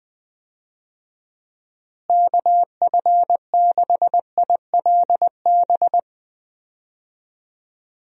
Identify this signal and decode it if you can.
Morse code, heard as a keyed tone, about 20 wpm, 700 Hz. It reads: KF6ILB
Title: USA Call Signs - 6 Characters - Repeated 2x - 20wpm